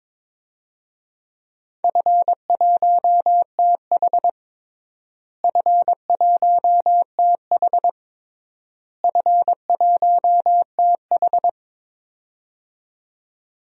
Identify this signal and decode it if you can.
Morse code, heard as a keyed tone, about 21 wpm, 700 Hz. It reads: F1TH F1TH F1TH
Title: DXCC-100 Call Signs - 4 Characters - Repeated 3x - 22wpm